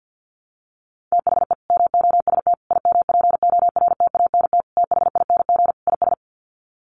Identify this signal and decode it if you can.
Morse code, heard as a keyed tone, about 50 wpm, 700 Hz. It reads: THE MOST IMPORTANT THING IS
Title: Sets of 5 Words Encore 50wpm